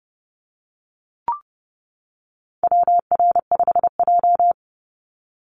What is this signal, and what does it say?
Morse code, heard as a keyed tone, about 30 wpm, 700 Hz. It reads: WR5J
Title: Call Signs - 30wpm